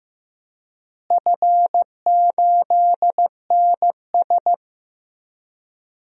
Morse code, heard as a keyed tone, about 15 wpm, 700 Hz. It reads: F8NS
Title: DXCC-100 Call Signs - 4 Characters - Repeated 2x - 15wpm